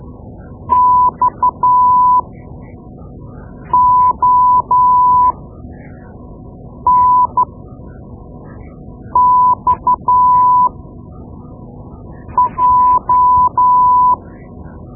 Morse code, heard as a keyed tone, about 12 wpm, spaced as for 4 wpm, 1000 Hz. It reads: XONXJ
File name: Morse Lessons / Morse Lessons / Tape-2B